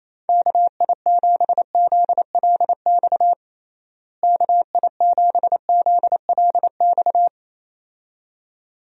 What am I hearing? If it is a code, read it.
Morse code, heard as a keyed tone, about 28 wpm, 700 Hz. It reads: KI7ZLX KI7ZLX
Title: USA Call Signs - 6 Characters - Repeated 2x - 28wpm